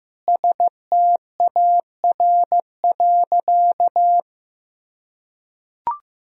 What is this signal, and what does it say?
Morse code, heard as a keyed tone, about 15 wpm, 700 Hz. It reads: STAR.
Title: Sentences from Top 1000 Words - Review Part 2 - 15wpm